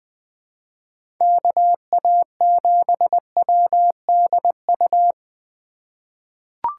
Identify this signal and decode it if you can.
Morse code, heard as a keyed tone, about 20 wpm, 700 Hz. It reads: KA7WDU
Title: USA Call Signs - 6 Characters - 4x Character Spacing - 20wpm